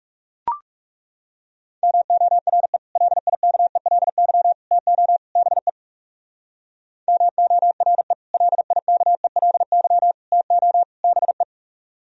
Morse code, heard as a keyed tone, about 36 wpm, 690 Hz. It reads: MORE LIKELY TO BE MORE LIKELY TO BE
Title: Speed-Racing - Sets of 4 Words 30wpm